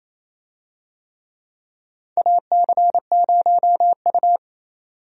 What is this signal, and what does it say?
Morse code, heard as a keyed tone, about 28 wpm, 700 Hz. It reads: AC0U
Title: USA Call Signs - 4 Characters - 28wpm